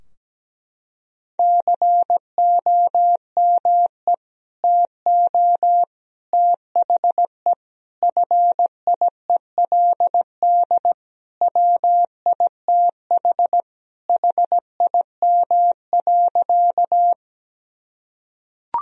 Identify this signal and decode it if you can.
Morse code, heard as a keyed tone, about 17 wpm, 700 Hz. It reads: COME TO THE FIELD WITH HIM.
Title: Sentences from Top 1000 Words - Review Part 2 - 17wpm